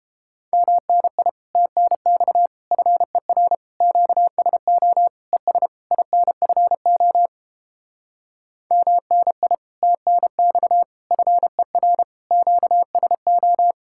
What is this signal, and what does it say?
Morse code, heard as a keyed tone, about 29 wpm, 700 Hz. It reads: MNI TNX FER QSO ES INFO MNI TNX FER QSO